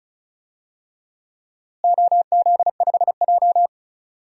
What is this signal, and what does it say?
Morse code, heard as a keyed tone, about 36 wpm, 700 Hz. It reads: OZ5J
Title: DXCC-100 Call Signs - 4 Characters - Repeated 2x - 35wpm